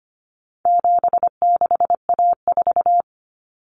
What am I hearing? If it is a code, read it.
Morse code, heard as a keyed tone, about 25 wpm, 700 Hz. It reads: 76A4
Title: Common Cut Numbers - 4-Digit 25wpm